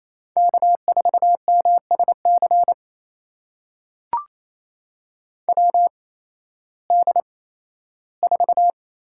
Morse code, heard as a keyed tone, about 28 wpm, spaced as for 15 wpm, 700 Hz. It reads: K4MSC WD4
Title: USA Call Signs - 8x Character Spacing - 28wpm